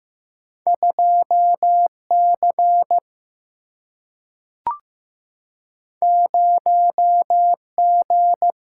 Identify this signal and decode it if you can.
Morse code, heard as a keyed tone, about 15 wpm, 700 Hz. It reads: 2 C 0 G
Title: Single-Letter Single-Number 15wpm